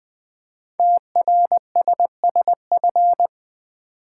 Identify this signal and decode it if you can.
Morse code, heard as a keyed tone, about 20 wpm, 700 Hz. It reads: TRSSF
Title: Code Groups - 5 Letters - 20wpm